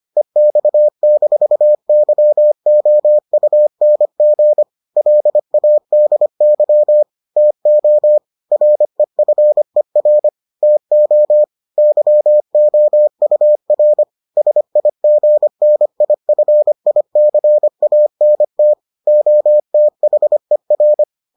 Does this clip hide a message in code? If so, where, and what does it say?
Yes: Morse code from 0.2 to 21.0 s, heard as a keyed tone, about 25 wpm, 600 Hz. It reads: EX-YOUNG LADY TO REFER TO YOUR SIGNIFICANT OTHER